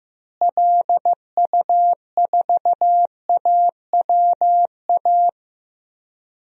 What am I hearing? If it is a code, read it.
Morse code, heard as a keyed tone, about 15 wpm, 700 Hz. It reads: LU4AWA
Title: DXCC-100 Call Signs - 6 Characters - Repeated 3x - 15wpm